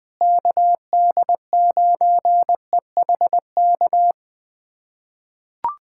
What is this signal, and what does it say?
Morse code, heard as a keyed tone, about 20 wpm, 700 Hz. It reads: KD9EHK
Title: USA Callsign DE Callsign - 6 Characters - 20wpm